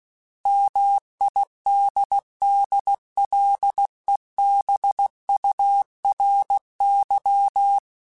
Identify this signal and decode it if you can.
Morse code, heard as a keyed tone, about 16 wpm, 800 Hz. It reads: MIDDLEBURY
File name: fromSMs / FromSMs mp3s / Cities@15